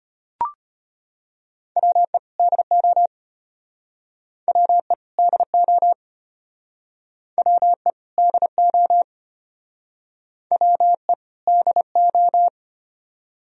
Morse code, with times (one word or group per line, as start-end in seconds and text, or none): 1.8-2.2 WE
2.4-3.1 DO
4.5-4.9 WE
5.2-5.9 DO
7.4-7.9 WE
8.2-9.0 DO
10.5-11.1 WE
11.5-12.5 DO